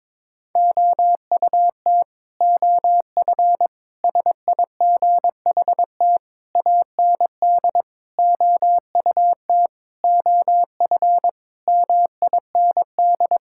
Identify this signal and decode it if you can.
Morse code, heard as a keyed tone, about 22 wpm, 700 Hz. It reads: OUT OF SIGHT AND OUT OF MIND
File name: practice / 22wpm / Sets of 7 Words - 22wpm